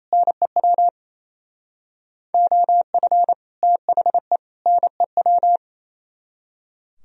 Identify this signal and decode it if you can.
Morse code, heard as a keyed tone, about 37 wpm, 700 Hz. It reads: NEW OF THE NEW